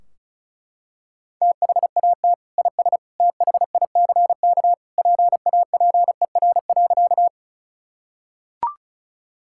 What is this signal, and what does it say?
Morse code, heard as a keyed tone, about 36 wpm, 700 Hz. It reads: THAT IS THICK PAPER.
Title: Sentences from Top 1600 Words 35wpm